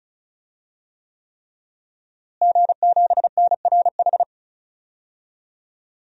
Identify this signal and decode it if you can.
Morse code, heard as a keyed tone, about 36 wpm, 700 Hz. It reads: G7NRH
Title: DXCC-100 Call Signs - 5 Characters - Repeated 3x - 35wpm